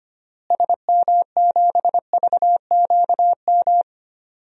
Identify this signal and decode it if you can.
Morse code, heard as a keyed tone, about 25 wpm, 700 Hz. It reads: SM7VQM
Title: DXCC-100 Call Signs - 6 Characters - Repeated 2x - 25wpm